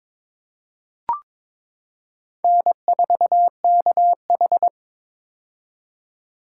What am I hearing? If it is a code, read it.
Morse code, heard as a keyed tone, about 22 wpm, 700 Hz. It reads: N4KH
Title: K1USN SST QSO Practice - 22wpm